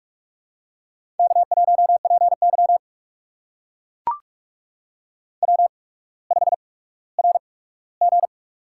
Morse code, heard as a keyed tone, about 44 wpm, spaced as for 19 wpm, 700 Hz. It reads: K1PY W5RG
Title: USA Call Signs - 4 Characters - 8x Character Spacing - 45wpm